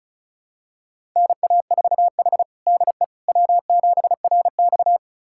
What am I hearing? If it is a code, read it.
Morse code, heard as a keyed tone, about 36 wpm, 700 Hz. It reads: NA4H DE W7RX
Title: USA Callsign DE Callsign - 4 Characters - 35wpm